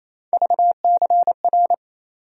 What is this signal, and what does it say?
Morse code, heard as a keyed tone, about 28 wpm, 700 Hz. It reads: VCR
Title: USA Call Sign Postfixes - 3 Letters 28wpm